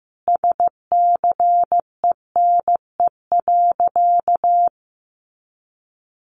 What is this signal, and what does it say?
Morse code, heard as a keyed tone, about 15 wpm, 700 Hz. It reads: SCENE.
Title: Sentences from Top 1300 Words 15wpm